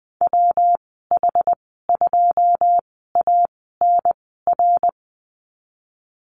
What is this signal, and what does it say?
Morse code, heard as a keyed tone, about 20 wpm, spaced as for 15 wpm, 700 Hz. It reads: WH2ANR
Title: 122 Lesson 20g - Number 3 - Call Signs 20wpm